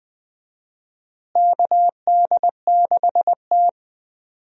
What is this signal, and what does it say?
Morse code, heard as a keyed tone, about 20 wpm, 700 Hz. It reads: KD6T
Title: USA Call Signs - 4 Characters - Repeated 2x - 20wpm